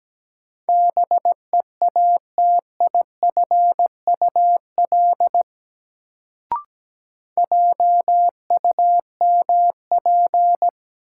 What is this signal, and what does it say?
Morse code, heard as a keyed tone, about 17 wpm, 700 Hz. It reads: BEATIFUL JUMP
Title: Sentences from Top 1300 Words - New Word Drill Alt - 17wpm